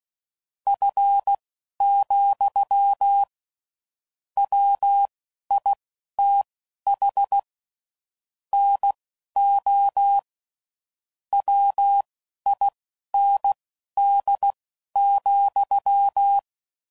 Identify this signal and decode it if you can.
Morse code, heard as a keyed tone, about 16 wpm, spaced as for 11 wpm, 800 Hz. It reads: F, WITH NO WIND,
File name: CW Code Groups @10WPM / Weather Sets / Weather 3 17